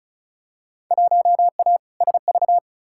Morse code, heard as a keyed tone, about 36 wpm, 700 Hz. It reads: 1A SV
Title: Field Day Exchanges - 35wpm